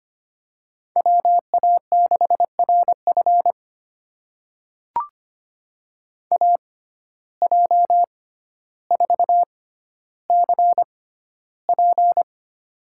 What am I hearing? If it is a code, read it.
Morse code, heard as a keyed tone, about 25 wpm, spaced as for 13 wpm, 700 Hz. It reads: WA6RF AJ4CP
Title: USA Call Signs - 5 Characters - 6x Character Spacing - 25wpm